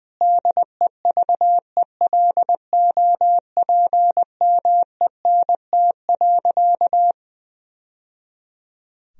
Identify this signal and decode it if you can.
Morse code, heard as a keyed tone, about 20 wpm, 700 Hz. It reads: DEVELOPMENT.
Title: Sentences from Top 1000 Words - Review Part 4 - 20wpm